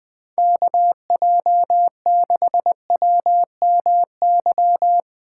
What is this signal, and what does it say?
Morse code, heard as a keyed tone, about 20 wpm, 700 Hz. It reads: KJ6WMY
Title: USA Call Signs - 6 Characters - 4x Character Spacing - 20wpm